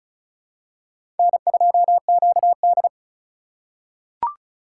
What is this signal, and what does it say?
Morse code, heard as a keyed tone, about 36 wpm, 700 Hz. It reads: N2QD
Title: USA Call Signs - 4 Characters - 3x Character Spacing - 35wpm